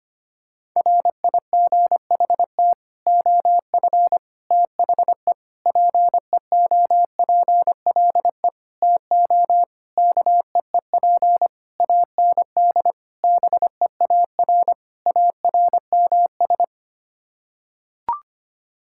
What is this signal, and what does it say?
Morse code, heard as a keyed tone, about 25 wpm, 700 Hz. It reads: RIGHT OF THE PEOPLE TO KEEP AND BEAR ARMS
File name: Sets of 9 Words - 25wpm